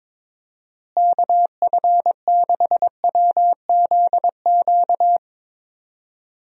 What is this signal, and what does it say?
Morse code, heard as a keyed tone, about 22 wpm, 700 Hz. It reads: KF6WZQ
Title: USA Call Signs - 6 Characters - Repeated 3x - 22wpm